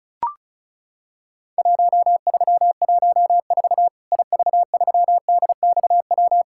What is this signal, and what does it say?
Morse code, heard as a keyed tone, about 36 wpm, 700 Hz. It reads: 1314 IV3DXW
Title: Worked All Europe DX Contest - QTC Exchange - 35wpm